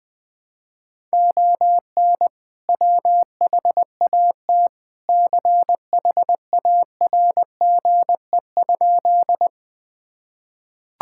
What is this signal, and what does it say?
Morse code, heard as a keyed tone, about 21 wpm, 700 Hz. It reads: ON WHAT CHARGE?
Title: Sentences from Top 1000 Words - Review Part 6 - 20wpm